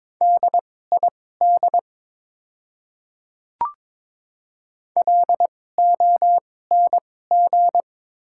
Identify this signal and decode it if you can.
Morse code, heard as a keyed tone, about 22 wpm, spaced as for 16 wpm, 700 Hz. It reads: DID LONG
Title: 2x Character Spacing - Top 200 Words - 22wpm